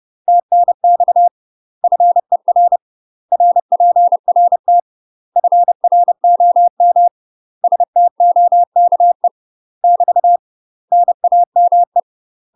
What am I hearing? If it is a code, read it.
Morse code, heard as a keyed tone, about 30 wpm, 700 Hz. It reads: TNX FER RPRT FROM STOKE <BT> NAME
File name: UK MASTER QSOs TRAINING / 30WPM / UK MASTER QSOs 0008